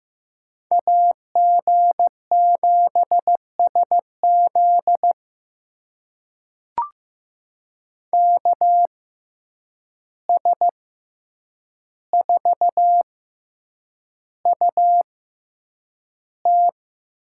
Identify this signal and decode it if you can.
Morse code, heard as a keyed tone, about 15 wpm, spaced as for 8 wpm, 700 Hz. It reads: AG7SZ KS4UT
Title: USA Call Signs - 5 Characters - 6x Character Spacing - 15wpm